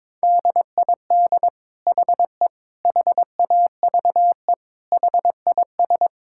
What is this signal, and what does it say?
Morse code, heard as a keyed tone, about 22 wpm, 700 Hz. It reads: DID HE HAVE HIS
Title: Sentences from Top 1000 Words - Review Part 5 - 22wpm